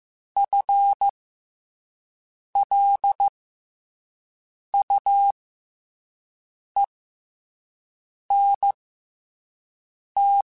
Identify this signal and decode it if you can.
Morse code, heard as a keyed tone, about 15 wpm, spaced as for 5 wpm, 800 Hz. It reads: FLUENT